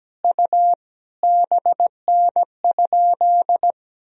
Morse code, heard as a keyed tone, about 17 wpm, 700 Hz. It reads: U BN?
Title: Sentences from Top 400 Words and Abbreviations 17wpm